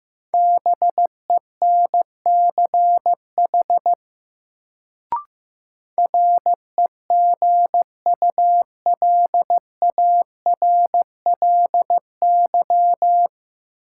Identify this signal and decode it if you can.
Morse code, heard as a keyed tone, about 15 wpm, 700 Hz. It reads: BENCH REGULARLY
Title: Sentences from Top 1400 Words - New Word Drill - 15wpm